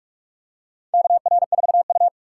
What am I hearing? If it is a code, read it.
Morse code, heard as a keyed tone, about 45 wpm, 700 Hz. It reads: KR4U
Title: USA Call Signs - 4 Characters - 2x Character Spacing - 45wpm